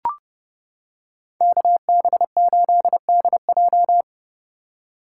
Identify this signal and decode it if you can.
Morse code, heard as a keyed tone, about 29 wpm, 700 Hz. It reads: KB8DJ